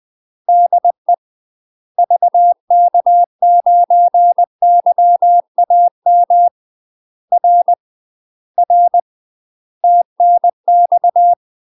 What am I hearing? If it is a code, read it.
Morse code, heard as a keyed tone, about 20 wpm, 700 Hz. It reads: DE VK9YAM R R TNX
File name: UK MASTER QSOs TRAINING/20WPM